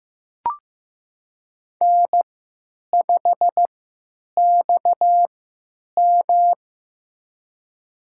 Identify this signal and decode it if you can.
Morse code, heard as a keyed tone, about 15 wpm, spaced as for 9 wpm, 700 Hz. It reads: N5XM
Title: USA Call Signs - 3x Character Spacing - 15wpm